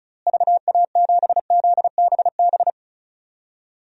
Morse code, heard as a keyed tone, about 36 wpm, 700 Hz. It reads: VA7ZBB
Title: DXCC-100 Call Signs - 6 Characters - Repeated 2x - 35wpm